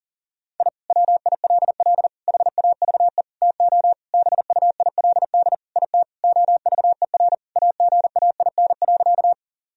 Morse code, heard as a keyed tone, about 40 wpm, 700 Hz. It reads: I WILL HAVE TO BUILD IT OVER AGAIN.
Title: Sentences from Top 1000 Words - Review Part 1 - 40wpm